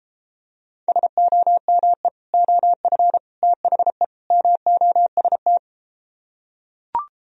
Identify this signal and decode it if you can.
Morse code, heard as a keyed tone, about 33 wpm, 700 Hz. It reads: SOME OF THE MOST